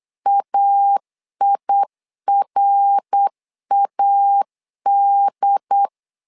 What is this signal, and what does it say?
Morse code, heard as a keyed tone, about 8 wpm, 800 Hz. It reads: AIRAD
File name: LEZIONI / 4 Q 8wpm 2732